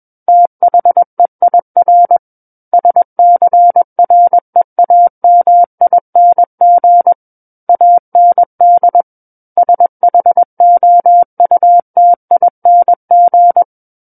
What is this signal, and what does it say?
Morse code, heard as a keyed tone, about 21 wpm, 700 Hz. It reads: THEIR SCREAMING AND SHOUTING
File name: Book One / 12-Chapter 12 - 21 WPM